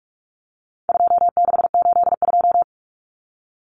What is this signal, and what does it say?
Morse code, heard as a keyed tone, about 44 wpm, 700 Hz. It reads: 2682
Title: Common Cut Numbers - 4-Digit 45wpm